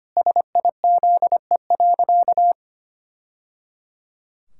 Morse code, heard as a keyed tone, about 25 wpm, 700 Hz. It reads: SIZE.
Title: Sentences from Top 1000 Words - Review Part 6 - 25wpm